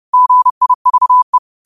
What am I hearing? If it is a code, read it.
Morse code, heard as a keyed tone, about 30 wpm, 1000 Hz. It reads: GIVE
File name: qotdfiles / 30wpm